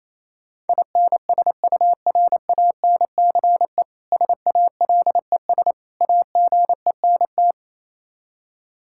Morse code, heard as a keyed tone, about 27 wpm, 700 Hz. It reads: INSURANCE SALES AGENT
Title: K1USN SST QSO Practice 28wpm